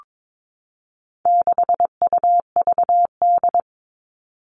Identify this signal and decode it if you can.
Morse code, heard as a keyed tone, about 22 wpm, 700 Hz. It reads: 6UVD
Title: All Cut Numbers - 4-Digit 22wpm